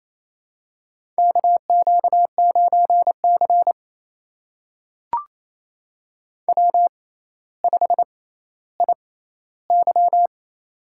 Morse code, heard as a keyed tone, about 27 wpm, spaced as for 14 wpm, 700 Hz. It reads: KQ9C W5IY
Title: USA Call Signs - 4 Characters - 6x Character Spacing - 28wpm